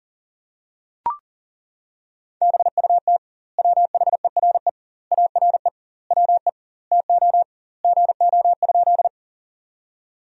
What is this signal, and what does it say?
Morse code, heard as a keyed tone, about 40 wpm, 700 Hz. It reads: BUT WHERE ARE WE TO GO?
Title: 2x Word-Spacing - Sentences from Top 200 Words - 40wpm